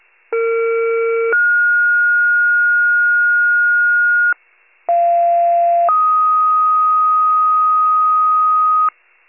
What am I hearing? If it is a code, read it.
Two-tone page, heard as a one-second tone, then a three-second tone, about 460 then 1430 Hz, about 690 then 1250 Hz, with noise. It is heard from 0.3 to 8.9 s.
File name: pager / 21 FIRE / wcalert